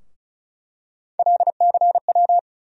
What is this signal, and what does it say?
Morse code, heard as a keyed tone, about 34 wpm, 700 Hz. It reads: LCW
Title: USA Call Sign Postfixes - 3 Letters 35wpm